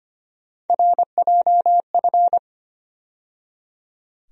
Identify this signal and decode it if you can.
Morse code, heard as a keyed tone, about 25 wpm, 700 Hz. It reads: RJF
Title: USA Call Sign Postfixes - 3 Letters 25wpm